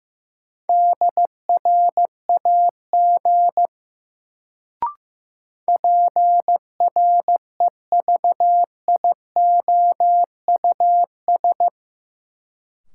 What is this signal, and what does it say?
Morse code, heard as a keyed tone, about 15 wpm, 700 Hz. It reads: DRAG PREVIOUS
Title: Sentences from Top 1500 Words - New Word Drill Alt - 15wpm